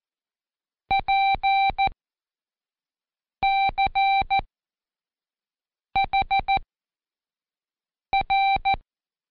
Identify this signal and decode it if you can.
Morse code, heard as a keyed tone, about 14 wpm, spaced as for 5 wpm, 780 Hz. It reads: PCHR
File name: Code Quick / AudioDownld1 / Seg37-Day4-Quiz CHLN